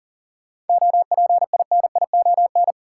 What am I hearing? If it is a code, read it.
Morse code, heard as a keyed tone, about 40 wpm, 700 Hz. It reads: OPINION
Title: Sentences from Top 1200 Words - New Word Drill Alt - 40wpm